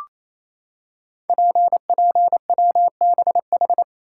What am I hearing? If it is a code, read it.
Morse code, heard as a keyed tone, about 27 wpm, 700 Hz. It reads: PPWBH
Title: Code Groups - 5 Letters - 28wpm